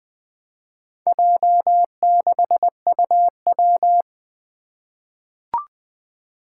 Morse code, heard as a keyed tone, about 20 wpm, 700 Hz. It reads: J6UW
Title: Four Challenging Letter-Number Combinations 20wpm